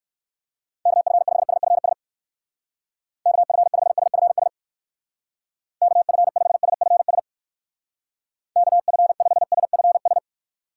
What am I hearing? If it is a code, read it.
Morse code, heard as a keyed tone, about 68 wpm, 700 Hz. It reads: KF5SFS KF5SFS KF5SFS KF5SFS